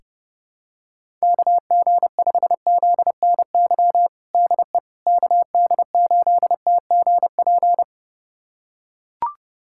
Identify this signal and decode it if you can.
Morse code, heard as a keyed tone, about 30 wpm, 700 Hz. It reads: KG5ZNY DE KD8TGP
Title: USA Callsign DE Callsign - 6 Characters - 30wpm